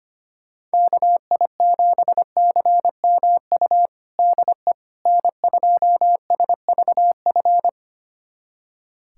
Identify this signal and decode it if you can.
Morse code, heard as a keyed tone, about 25 wpm, 700 Hz. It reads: KI7CMU DE N2SVF
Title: USA Callsign DE Callsign 25wpm